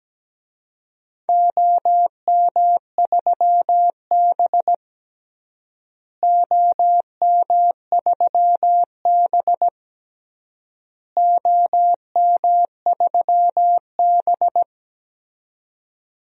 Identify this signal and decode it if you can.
Morse code, heard as a keyed tone, about 17 wpm, 700 Hz. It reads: OM3B OM3B OM3B
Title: DXCC-100 Call Signs - 4 Characters - Repeated 3x - 17wpm